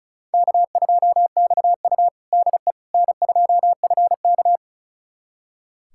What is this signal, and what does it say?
Morse code, heard as a keyed tone, about 36 wpm, 700 Hz. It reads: K2XU DE N2FK
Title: USA Callsign DE Callsign - 4 Characters - 35wpm